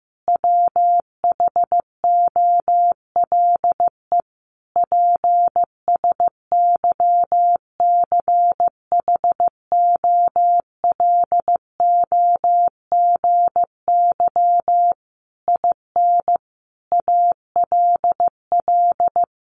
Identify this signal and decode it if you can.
Morse code, heard as a keyed tone, about 15 wpm, 700 Hz. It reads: WHOLE PSYCHOLOGY IN ALL
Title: Sentences from Top 1400 Words 15wpm